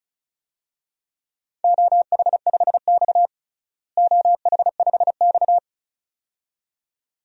Morse code, heard as a keyed tone, about 36 wpm, 700 Hz. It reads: OH5X OH5X
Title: DXCC-100 Call Signs - 4 Characters - Repeated 2x - 35wpm